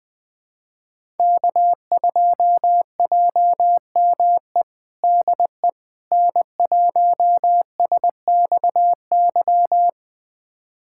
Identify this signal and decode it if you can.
Morse code, heard as a keyed tone, about 20 wpm, 700 Hz. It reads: K2JME DE N1SXY
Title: USA Callsign DE Callsign - 5 Characters - 20wpm